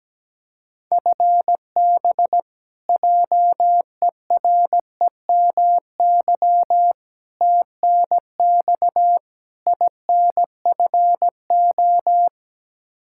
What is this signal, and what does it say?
Morse code, heard as a keyed tone, about 17 wpm, 700 Hz. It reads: FB JEREMY TNX INFO
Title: Practice QSO - Exchange 3 - More Info - 17wpm